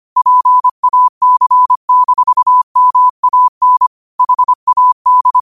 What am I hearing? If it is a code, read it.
Morse code, heard as a keyed tone, about 25 wpm, 1000 Hz. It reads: PAC-MAN HAD
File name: qotdfiles / 25wpm